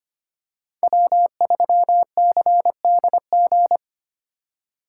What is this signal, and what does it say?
Morse code, heard as a keyed tone, about 25 wpm, 700 Hz. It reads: W3CDG
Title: USA Call Signs - 3x Character Spacing - 25wpm